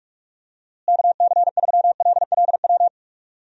Morse code, heard as a keyed tone, about 47 wpm, 700 Hz. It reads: KC3LLW
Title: USA Call Signs - 6 Characters - Repeated 3x - 45wpm